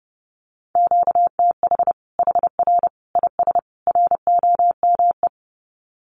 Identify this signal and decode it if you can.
Morse code, heard as a keyed tone, about 30 wpm, 700 Hz. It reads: QTH HR IS ROME
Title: Common QSO Phrases 30wpm